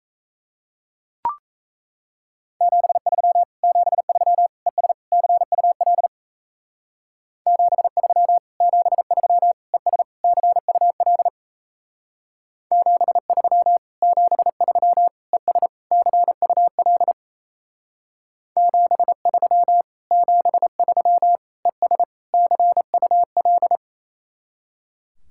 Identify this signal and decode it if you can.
Morse code, heard as a keyed tone, about 33 wpm, 700 Hz. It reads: 73 73 ES CUL 73 73 ES CUL 73 73 ES CUL 73 73 ES CUL
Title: Speed-Racing Common QSO Phrases - 28wpm